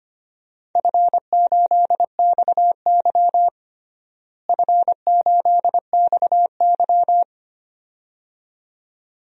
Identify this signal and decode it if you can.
Morse code, heard as a keyed tone, about 25 wpm, 700 Hz. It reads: F8XY F8XY
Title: DXCC-100 Call Signs - 4 Characters - Repeated 2x - 25wpm